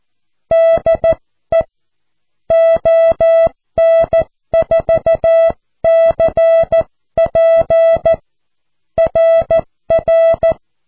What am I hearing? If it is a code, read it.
Morse code, heard as a keyed tone, about 14 wpm, 660 Hz. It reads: DE ON4CP RR
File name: cw / QSO